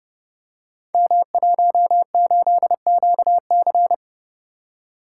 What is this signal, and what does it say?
Morse code, heard as a keyed tone, about 31 wpm, 700 Hz. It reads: M18QC
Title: Code Groups - 5 Letter:Numbers - 30wpm